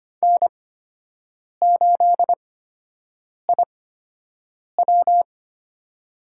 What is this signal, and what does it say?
Morse code, heard as a keyed tone, about 25 wpm, spaced as for 7 wpm, 700 Hz. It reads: N8IW
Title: USA Call Signs - 8x Character Spacing - 25wpm